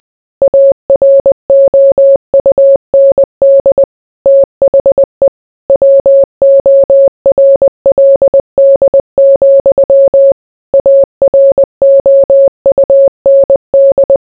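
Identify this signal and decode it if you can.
Morse code, heard as a keyed tone, about 20 wpm, 550 Hz. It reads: AROUND THE WORLD, AROUND
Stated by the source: Morse code mixed with noise and music as background sound for sleep.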